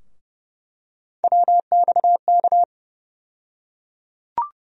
Morse code, heard as a keyed tone, about 30 wpm, 700 Hz. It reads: WXK
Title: USA Call Sign Postfixes - 3 Letters 30wpm